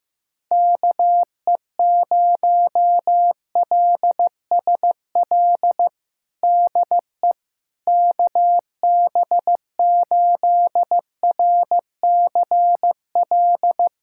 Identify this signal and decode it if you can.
Morse code, heard as a keyed tone, about 15 wpm, 700 Hz. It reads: KE0LSL DE KB8RCL
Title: USA Callsign DE Callsign 15wpm